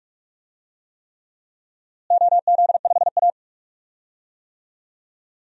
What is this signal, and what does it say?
Morse code, heard as a keyed tone, about 46 wpm, 700 Hz. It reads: OZ5A
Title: DXCC-100 Call Signs - 4 Characters - Repeated 3x - 45wpm